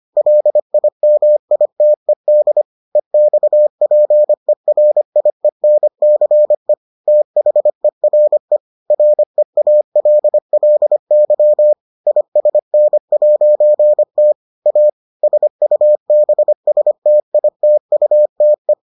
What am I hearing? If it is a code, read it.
Morse code, heard as a keyed tone, about 25 wpm, 600 Hz. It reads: LIMITED EXPERIENCE THERE REALLY ISN'T A SUBSTITUTE